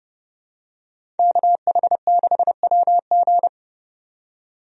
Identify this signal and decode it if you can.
Morse code, heard as a keyed tone, about 31 wpm, 700 Hz. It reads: KH6WG
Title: DXCC-100 Call Signs - 5 Characters - 30wpm